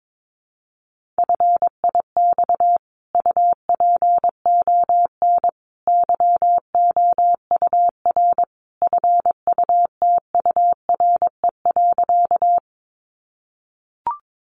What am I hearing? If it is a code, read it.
Morse code, heard as a keyed tone, about 22 wpm, 700 Hz. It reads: FIX UPON YOUR FUTURE.
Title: Sentences from Top 1400 Words 22wpm